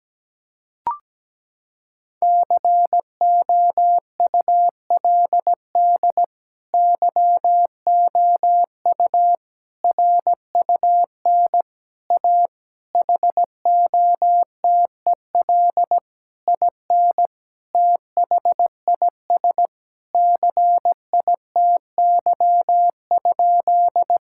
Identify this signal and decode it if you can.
Morse code, heard as a keyed tone, about 17 wpm, 700 Hz. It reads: COULD YOU RUN A HOTEL IN THIS CITY?
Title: Sentences from Top 1600 Words 17wpm